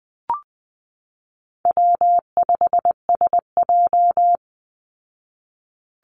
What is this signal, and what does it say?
Morse code, heard as a keyed tone, about 20 wpm, 700 Hz. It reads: W5SJ
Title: Easy Call Signs Repeated 3x - 20wpm